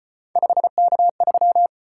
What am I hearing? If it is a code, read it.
Morse code, heard as a keyed tone, about 34 wpm, 700 Hz. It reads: 5K3